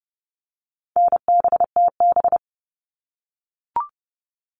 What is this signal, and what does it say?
Morse code, heard as a keyed tone, about 30 wpm, 700 Hz. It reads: NBTB
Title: All Cut Numbers - 4-Digit 30wpm